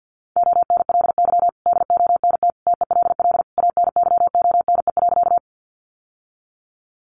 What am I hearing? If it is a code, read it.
Morse code, heard as a keyed tone, about 48 wpm, 700 Hz. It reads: ONLY DONT TELL ANYONE.
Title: Sentences from Top 900 Words 50wpm